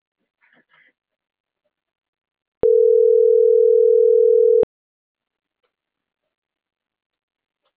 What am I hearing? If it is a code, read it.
Ringback tone (North America), 440 + 480 Hz.